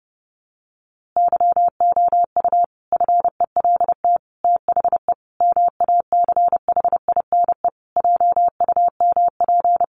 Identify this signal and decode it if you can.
Morse code, heard as a keyed tone, about 30 wpm, 700 Hz. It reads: YOU FELT THE MACHINE JUMP
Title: A Sound Of Thunder 30wpm